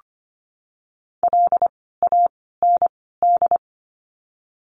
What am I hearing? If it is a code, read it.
Morse code, heard as a keyed tone, about 25 wpm, 700 Hz. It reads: L A N D
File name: course / 25:10wpm / 094 Lesson 16f - Letter c - Words 25wpm